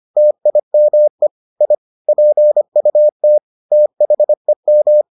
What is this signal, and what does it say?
Morse code, heard as a keyed tone, about 25 wpm, 600 Hz. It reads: TIME I PUT THEM